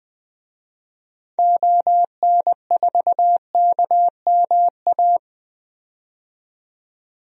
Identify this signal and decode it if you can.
Morse code, heard as a keyed tone, about 20 wpm, 700 Hz. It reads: ON4KMA
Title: DXCC-100 Call Signs - 6 Characters - 20wpm